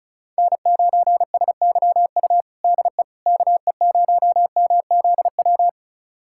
Morse code, heard as a keyed tone, about 36 wpm, 700 Hz. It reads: N9SYU DE KE0MZW
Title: USA Callsign DE Callsign 35wpm